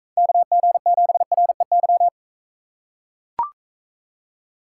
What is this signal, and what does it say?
Morse code, heard as a keyed tone, about 42 wpm, 700 Hz. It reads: KG7REY